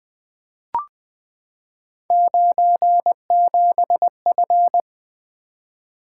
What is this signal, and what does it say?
Morse code, heard as a keyed tone, about 20 wpm, 700 Hz. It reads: 97F
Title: Three Letter-Number Combinations - 20wpm